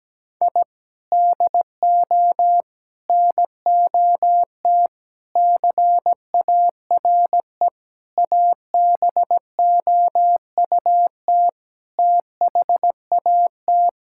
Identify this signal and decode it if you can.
Morse code, heard as a keyed tone, about 17 wpm, 700 Hz. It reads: I DO NOT CARE ABOUT THAT
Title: Sentences from Top 1000 Words - Review Part 6 - 17wpm